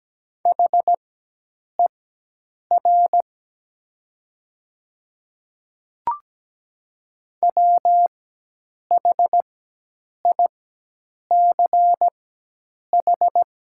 Morse code, heard as a keyed tone, about 17 wpm, spaced as for 8 wpm, 700 Hz. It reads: HER WHICH
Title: 4x Character Spacing - Top 100 Words - 17wpm